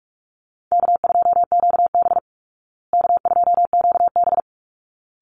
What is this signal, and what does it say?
Morse code, heard as a keyed tone, about 44 wpm, 700 Hz. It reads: K2QB K2QB
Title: Call Signs Repeated 3x - 45wpm